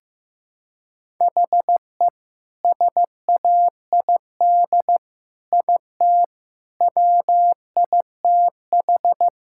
Morse code, heard as a keyed tone, about 15 wpm, 700 Hz. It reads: HE SAID IT WITH
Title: Sentences from Top 1600 Words 15wpm